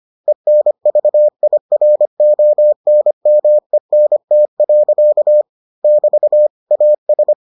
Morse code, heard as a keyed tone, about 25 wpm, 600 Hz. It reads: ENVIRONMENT. <BT> AS